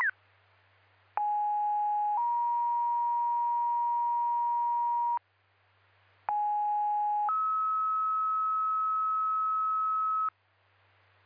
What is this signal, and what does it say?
Two-tone page, heard as a one-second tone, then a three-second tone, about 830 then 950 Hz, about 830 then 1290 Hz. It is heard from 1.2 to 10.3 s.